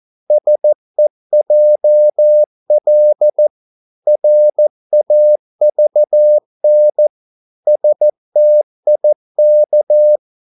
Morse code, heard as a keyed tone, about 14 wpm, 600 Hz. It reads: SEJL RAVN STIK
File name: Korte ord DK 14wpm